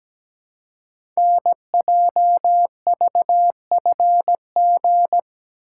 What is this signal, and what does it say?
Morse code, heard as a keyed tone, about 17 wpm, 700 Hz. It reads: NJVFG
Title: Code Groups - 5 Letters - 17wpm